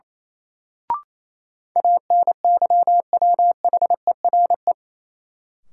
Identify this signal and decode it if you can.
Morse code, heard as a keyed tone, about 28 wpm, 700 Hz. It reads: ANYWHERE
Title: Sentences from Top 1400 Words - New Word Drill - 28wpm